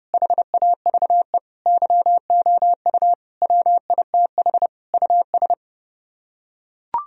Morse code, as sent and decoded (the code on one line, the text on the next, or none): .... .- ...- . / -.-- --- ..- / .-- .. - .... / ..- ...
HAVE YOU WITH US